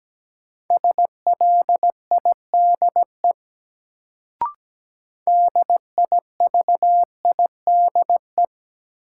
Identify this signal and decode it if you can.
Morse code, heard as a keyed tone, about 17 wpm, 700 Hz. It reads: SLIDE DIVIDE
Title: Sentences from Top 1600 Words - New Word Drill Alt - 17wpm